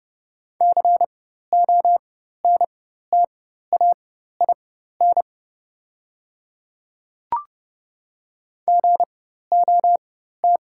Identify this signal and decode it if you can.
Morse code, heard as a keyed tone, about 30 wpm, spaced as for 14 wpm, 700 Hz. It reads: CONTAIN GOT
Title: 4x Character Spacing - Top 500 Words - 30wpm